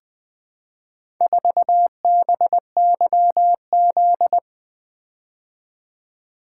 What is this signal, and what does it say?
Morse code, heard as a keyed tone, about 20 wpm, 700 Hz. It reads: 4BYZ
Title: Four Letter-Number Combinations - 20wpm